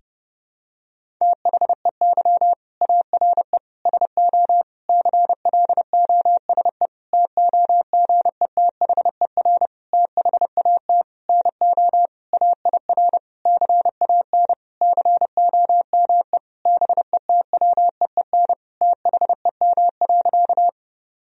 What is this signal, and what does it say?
Morse code, heard as a keyed tone, about 30 wpm, 700 Hz. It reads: THEY ARE SO CLOSE TOGETHER THAT NO AIR CAN COME BETWEEN THEM.
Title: Sentences from Top 1000 Words - Review Part 1 - 30wpm